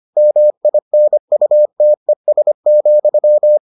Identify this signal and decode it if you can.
Morse code, heard as a keyed tone, about 25 wpm, 600 Hz. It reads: MINUTES,